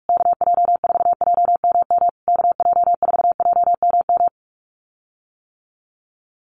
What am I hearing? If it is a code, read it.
Morse code, heard as a keyed tone, about 44 wpm, 700 Hz. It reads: KJ4JMM KJ4JMM
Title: USA Call Signs Repeated 3x 45wpm